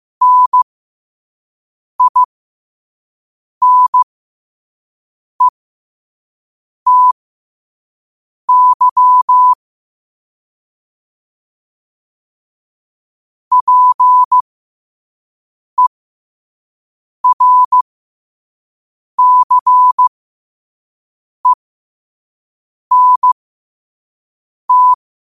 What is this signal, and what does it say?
Morse code, heard as a keyed tone, about 15 wpm, spaced as for 5 wpm, 1000 Hz. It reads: NINETY PERCENT